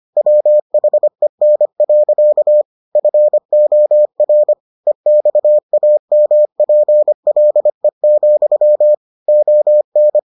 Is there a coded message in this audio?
Yes: Morse code from 0.2 to 10.2 s, heard as a keyed tone, about 25 wpm, 600 Hz. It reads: WHEN. FOR EXAMPLE, ON